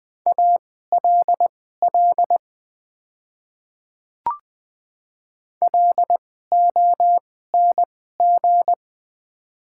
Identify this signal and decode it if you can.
Morse code, heard as a keyed tone, about 20 wpm, spaced as for 14 wpm, 700 Hz. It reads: ALL LONG
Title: 2x Character Spacing - Top 100 Words - 20wpm